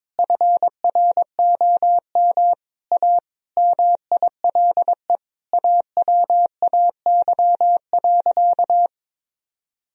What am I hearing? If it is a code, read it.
Morse code, heard as a keyed tone, about 22 wpm, 700 Hz. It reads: FROM A MILE AWAY.
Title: Sentences from Top 1000 Words - Review Part 3 - 22wpm